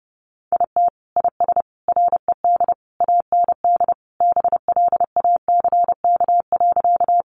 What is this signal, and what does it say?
Morse code, heard as a keyed tone, about 30 wpm, 700 Hz. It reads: IT IS RED AND BLACK.
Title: Easy Phrases 30wpm